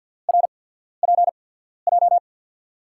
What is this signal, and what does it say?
Morse code, heard as a keyed tone, about 53 wpm, spaced as for 14 wpm, 700 Hz. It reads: RPJ
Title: USA Call Signs - 5 Characters - 8x Character Spacing - 50wpm